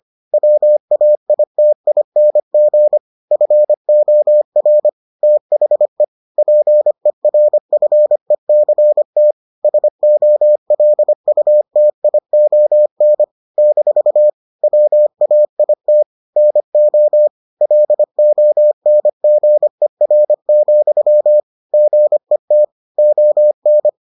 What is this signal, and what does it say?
Morse code, heard as a keyed tone, about 25 wpm, 600 Hz. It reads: WAITING FOR THE PERFECT SOLUTION - WAIT NO LONGER, GET ON